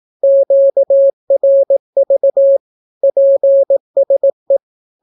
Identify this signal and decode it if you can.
Morse code, heard as a keyed tone, about 18 wpm, 550 Hz. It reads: QRV PSE
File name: naukapodstaw / skroty / skroty